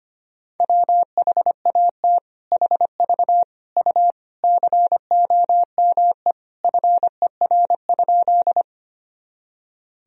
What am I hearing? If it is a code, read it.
Morse code, heard as a keyed tone, about 25 wpm, 700 Hz. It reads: WHAT HV U COME FER?
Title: Sentences from Top 100 Words and Abbreviations 25wpm